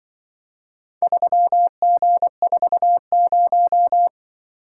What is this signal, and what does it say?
Morse code, heard as a keyed tone, about 23 wpm, 700 Hz. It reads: 3G40
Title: Speed-Racing - Four Letter-Number Combinations - 20wpm